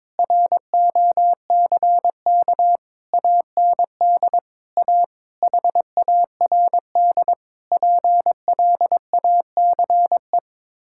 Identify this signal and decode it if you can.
Morse code, heard as a keyed tone, about 22 wpm, 700 Hz. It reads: ROCK AND A HARD PLACE
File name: practice / 22wpm / Sets of 8 Words - 22wpm